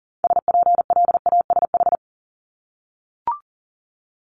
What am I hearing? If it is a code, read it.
Morse code, heard as a keyed tone, about 40 wpm, 700 Hz. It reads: SPLASH
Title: Multi-letter Phonemes Repeated 3x 40wpm